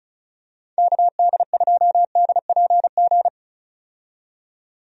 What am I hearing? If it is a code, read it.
Morse code, heard as a keyed tone, about 36 wpm, 700 Hz. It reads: KD2DPG
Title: USA Call Signs - 6 Characters - 35wpm